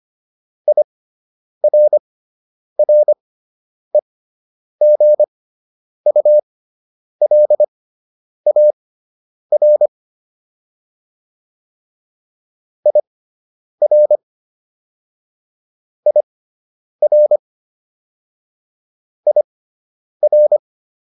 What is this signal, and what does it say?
Morse code, heard as a keyed tone, about 25 wpm, spaced as for 9 wpm, 600 Hz. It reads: IRREGULAR IR IR IR